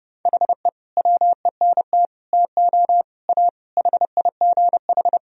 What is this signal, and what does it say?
Morse code, heard as a keyed tone, about 31 wpm, 700 Hz. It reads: HE WENT TO A HIGH
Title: Sentences from Top 1000 Words - Review Part 1 - 30wpm